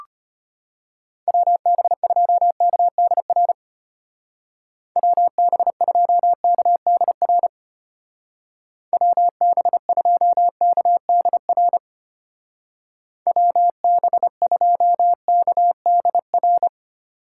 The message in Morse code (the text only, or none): WB2KDR WB2KDR WB2KDR WB2KDR